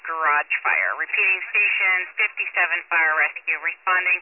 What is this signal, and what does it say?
Emergency Alert System (SAME) tones, heard as harsh buzzing bursts, with speech. It is heard from 0.7 to 3.4 s.